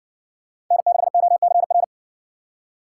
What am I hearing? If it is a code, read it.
Morse code, heard as a keyed tone, about 60 wpm, 700 Hz. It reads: N6OCR